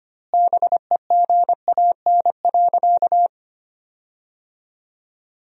Morse code, heard as a keyed tone, about 25 wpm, 700 Hz. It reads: BEGAN.
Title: 2x Word-Spacing - Sentences from Top 300 Words - 25wpm